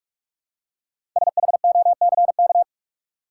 Morse code, heard as a keyed tone, about 44 wpm, 700 Hz. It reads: SHOCK
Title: Five Letter Words - 45wpm